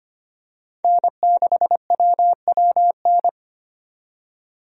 Morse code, heard as a keyed tone, about 25 wpm, 700 Hz. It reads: N6WWN
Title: USA Call Signs - 5 Characters - 25wpm